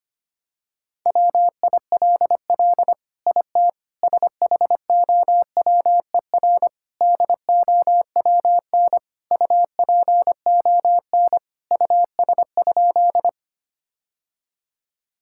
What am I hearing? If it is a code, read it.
Morse code, heard as a keyed tone, about 25 wpm, 700 Hz. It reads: WILL IT SHOWER DOWN UPON US?
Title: Sentences from Top 1600 Words 25wpm